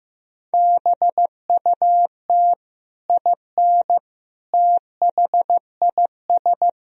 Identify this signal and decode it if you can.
Morse code, heard as a keyed tone, about 15 wpm, 700 Hz. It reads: BUT IN THIS
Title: Sentences from Top 1000 Words - Review Part 4 - 15wpm